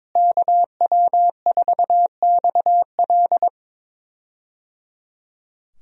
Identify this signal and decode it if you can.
Morse code, heard as a keyed tone, about 22 wpm, 700 Hz. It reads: KW4XL
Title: USA Call Signs - 5 Characters - Repeated 3x - 22wpm